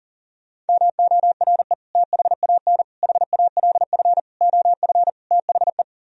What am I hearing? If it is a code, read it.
Morse code, heard as a keyed tone, about 52 wpm, 700 Hz. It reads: MORE THAN HALF OF THE